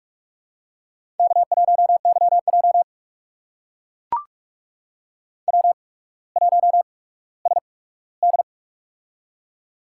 Morse code, heard as a keyed tone, about 44 wpm, spaced as for 19 wpm, 700 Hz. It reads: K1YJ W1SD
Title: USA Call Signs - 4 Characters - 8x Character Spacing - 45wpm